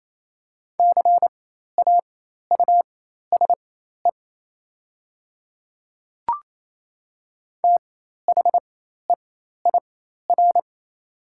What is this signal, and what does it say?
Morse code, heard as a keyed tone, about 27 wpm, spaced as for 13 wpm, 700 Hz. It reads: CAUSE THEIR
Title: 4x Character Spacing - Top 200 Words - 28wpm